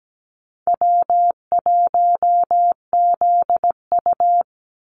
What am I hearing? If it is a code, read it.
Morse code, heard as a keyed tone, about 17 wpm, 700 Hz. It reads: W1ZU
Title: NA SOTA Activator Callsigns 17wpm